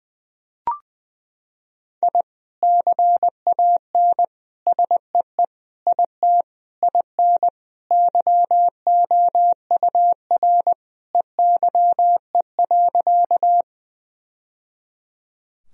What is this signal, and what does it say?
Morse code, heard as a keyed tone, about 20 wpm, 700 Hz. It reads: I CAN SEE IT IN YOUR EYE.
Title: Sentences from Top 1000 Words - Review Part 1 - 20wpm